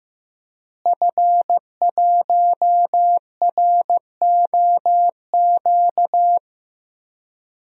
Morse code, heard as a keyed tone, about 15 wpm, 700 Hz. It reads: F1ROQ
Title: DXCC-100 Call Signs - 5 Characters - Repeated 3x - 15wpm